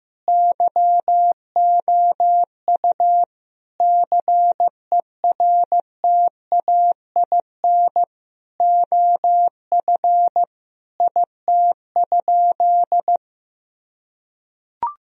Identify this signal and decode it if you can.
Morse code, heard as a keyed tone, about 16 wpm, 700 Hz. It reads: YOU CERTAIN OF IT?
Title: Sentences from Top 1000 Words - Review Part 3 - 15wpm